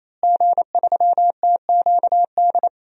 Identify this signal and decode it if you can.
Morse code, heard as a keyed tone, about 28 wpm, 700 Hz. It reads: G3TQD
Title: DXCC-100 Call Signs - 5 Characters - Repeated 3x - 28wpm